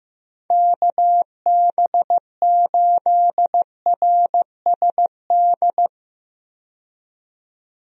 Morse code, heard as a keyed tone, about 15 wpm, 700 Hz. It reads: KB8RSD
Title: USA Call Signs - 6 Characters - 15wpm